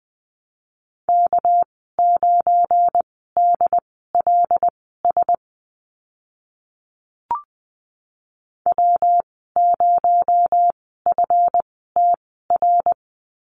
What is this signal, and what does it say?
Morse code, heard as a keyed tone, about 20 wpm, spaced as for 15 wpm, 700 Hz. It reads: K9DLS W0FTR
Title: 245 Lesson 39g - Letter z - Call Signs 20wpm